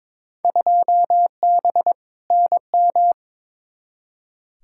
Morse code, heard as a keyed tone, about 22 wpm, 700 Hz. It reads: 2B NM
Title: Field Day Exchanges Encore - 22wpm